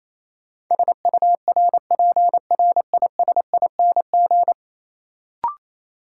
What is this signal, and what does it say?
Morse code, heard as a keyed tone, about 28 wpm, 700 Hz. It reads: SURPRISING
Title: Sets of 5 Words - New Word Drill - 28wpm